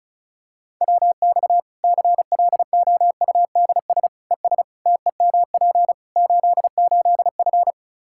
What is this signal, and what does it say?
Morse code, heard as a keyed tone, about 36 wpm, 700 Hz. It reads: WX CLOUDS ES TEMP 88F
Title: K1USN SST QSO Practice 35wpm